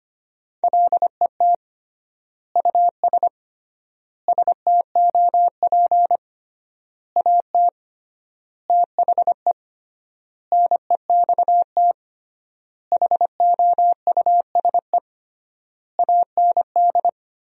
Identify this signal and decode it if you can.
Morse code, heard as a keyed tone, about 25 wpm, 700 Hz. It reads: LET US STOP AT THE NEXT HOUSE AND
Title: 3x Word-Spacing - Sentences from Top 400 Words - 25wpm